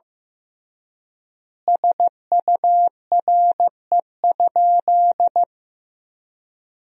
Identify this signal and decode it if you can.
Morse code, heard as a keyed tone, about 15 wpm, 700 Hz. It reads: SURE?
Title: 3x Word-Spacing - Sentences from Top 300 Words - 15wpm